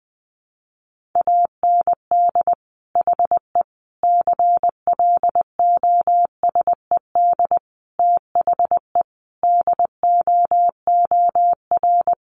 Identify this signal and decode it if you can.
Morse code, heard as a keyed tone, about 20 wpm, 700 Hz. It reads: AND HE CLOSED THE DOOR
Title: Sentences from Top 1500 Words 20wpm